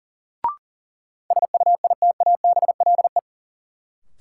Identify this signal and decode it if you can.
Morse code, heard as a keyed tone, about 40 wpm, 700 Hz. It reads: SUITABLE
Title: Sentences from Top 1400 Words - New Word Drill - 40wpm